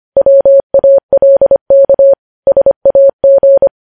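Morse code, heard as a keyed tone, about 25 wpm, 550 Hz. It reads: WALK SAG
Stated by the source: Morse code news headlines sent at 25.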